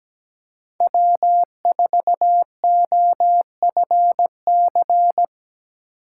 Morse code, heard as a keyed tone, about 17 wpm, 700 Hz. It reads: W4OFC
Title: USA Call Signs - 5 Characters - 4x Character Spacing - 17wpm